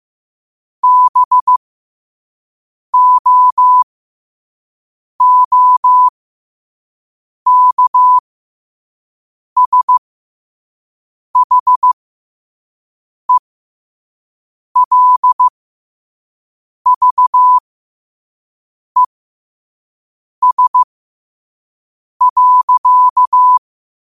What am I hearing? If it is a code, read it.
Morse code, heard as a keyed tone, about 15 wpm, spaced as for 5 wpm, 1000 Hz. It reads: BOOKSHELVES.
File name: qotdfiles / 5wpm